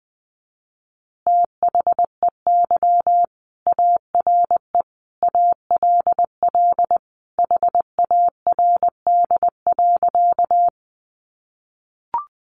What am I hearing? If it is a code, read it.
Morse code, heard as a keyed tone, about 20 wpm, 700 Hz. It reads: THEY ARE ALL HARD.
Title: Additional Easy Phrases 20wpm